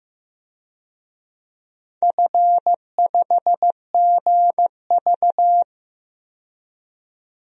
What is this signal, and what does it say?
Morse code, heard as a keyed tone, about 15 wpm, 700 Hz. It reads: F5GV
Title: DXCC-100 Call Signs - 4 Characters - 15wpm